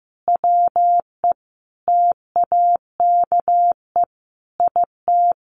Morse code, heard as a keyed tone, about 15 wpm, 700 Hz. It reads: WE TAKE IT
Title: A Sound Of Thunder 15wpm